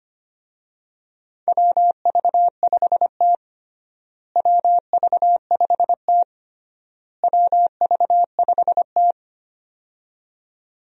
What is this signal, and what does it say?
Morse code, heard as a keyed tone, about 25 wpm, 700 Hz. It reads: WV5T WV5T WV5T
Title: USA Call Signs - 4 Characters - Repeated 3x - 25wpm